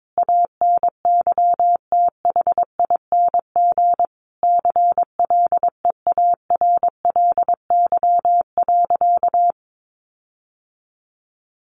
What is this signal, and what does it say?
Morse code, heard as a keyed tone, about 22 wpm, 700 Hz. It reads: ANYTHING CLEARLY.
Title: Sentences from Top 1300 Words 22wpm